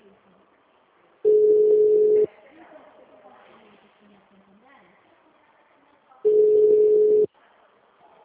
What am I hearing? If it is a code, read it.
Ringback tone (Europe, 425 Hz), with speech.
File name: out- -2016- - - .999